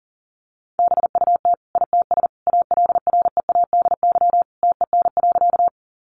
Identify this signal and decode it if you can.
Morse code, heard as a keyed tone, about 40 wpm, 700 Hz. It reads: BUT ITS ALREADY TEN.
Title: Sentences from Top 700 Words 40wpm